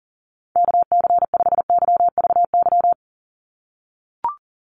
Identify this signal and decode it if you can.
Morse code, heard as a keyed tone, about 40 wpm, 700 Hz. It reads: KC5YVY
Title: USA Call Signs 40wpm